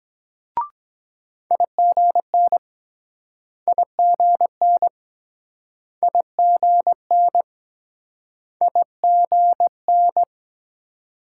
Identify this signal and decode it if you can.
Morse code, heard as a keyed tone, about 24 wpm, 700 Hz. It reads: IGN IGN IGN IGN